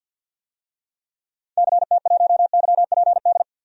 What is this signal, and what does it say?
Morse code, heard as a keyed tone, about 47 wpm, 700 Hz. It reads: CT1YPD
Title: DXCC-100 Call Signs - 6 Characters - 50wpm